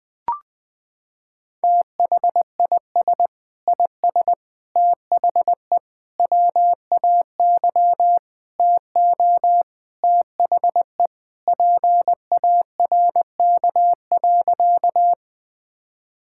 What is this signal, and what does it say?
Morse code, heard as a keyed tone, about 20 wpm, 700 Hz. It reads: THIS IS THE WAY TO THE PARK.
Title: Sentences with 4-Word Phrases - 20wpm